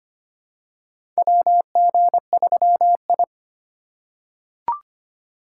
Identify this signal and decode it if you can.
Morse code, heard as a keyed tone, about 25 wpm, 700 Hz. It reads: WG3I
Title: USA Call Signs - 4 Characters - 2x Character Spacing - 25wpm